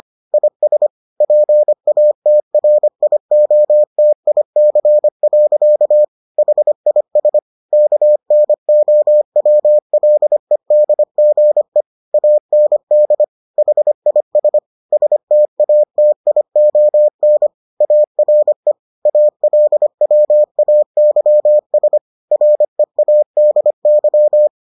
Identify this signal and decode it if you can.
Morse code, heard as a keyed tone, about 25 wpm, 600 Hz. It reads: IS PATRIOTIC. HIS KNOWLEDGE AND HIS STATION ARE ALWAYS READY